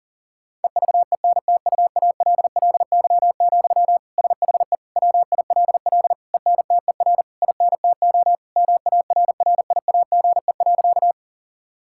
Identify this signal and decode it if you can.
Morse code, heard as a keyed tone, about 40 wpm, 690 Hz. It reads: EVENTUALLY, SHE WILL ENTER INTO MARRIAGE.
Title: Sentences from Top 1000 Words - Review Part 6 - 40wpm